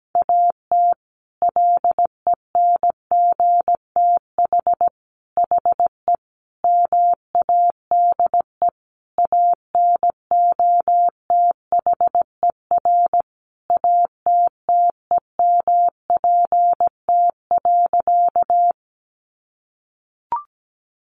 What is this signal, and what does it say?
Morse code, heard as a keyed tone, about 17 wpm, 700 Hz. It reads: AT LENGTH HE MADE ANOTHER ATTEMPT.
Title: Sentences from Top 1000 Words 17wpm